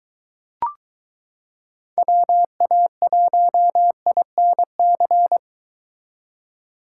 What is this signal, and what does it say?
Morse code, heard as a keyed tone, about 23 wpm, 700 Hz. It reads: WA1INC